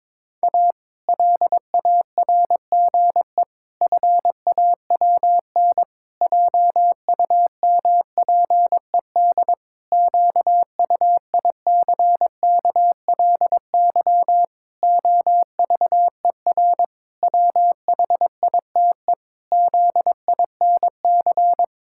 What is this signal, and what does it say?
Morse code, heard as a keyed tone, about 22 wpm, 700 Hz. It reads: A LARGE FAWN JUMPED QUICKLY OVER WHITE ZINC
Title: Pangram Challenge v2 - 22wpm